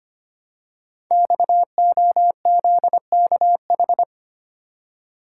Morse code, heard as a keyed tone, about 25 wpm, 700 Hz. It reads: XOZKH
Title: Code Groups - 5 Letter:Numbers - 25wpm